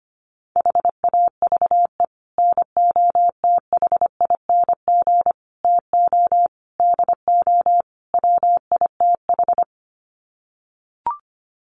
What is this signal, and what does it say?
Morse code, heard as a keyed tone, about 25 wpm, 700 Hz. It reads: HAVE NOTHING TO DO WITH
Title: Sets of 5 Words 25wpm